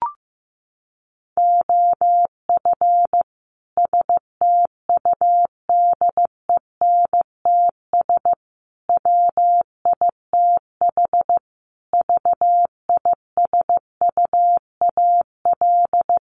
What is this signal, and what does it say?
Morse code, heard as a keyed tone, about 15 wpm, 700 Hz. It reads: OF STUDENTS WITH VISUAL
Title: Sets of 5 Words Encore 15wpm